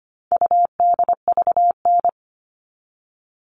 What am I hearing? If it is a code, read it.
Morse code, heard as a keyed tone, about 25 wpm, 700 Hz. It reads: UDVN
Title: All Cut Numbers - 4-Digit 25wpm